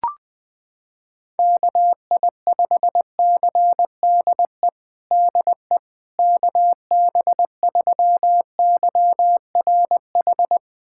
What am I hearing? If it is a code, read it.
Morse code, heard as a keyed tone, about 21 wpm, 700 Hz. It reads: KI5CDE DE KB3YRH
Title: USA Callsign DE Callsign 20wpm